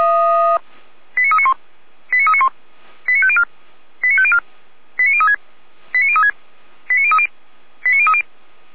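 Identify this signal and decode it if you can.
Five-tone selective call (ZVEI), heard as quick runs of five tones, tones of about 70 ms. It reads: 89391 89391 89504 89504 89036 89036 89030 89030
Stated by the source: ZVEI five-tone selective call received by SDR.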